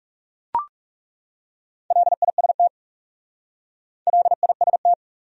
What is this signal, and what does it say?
Morse code, heard as a keyed tone, about 37 wpm, 700 Hz. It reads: LIST LIST
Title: Speed-Racing Top 500 Words 30wpm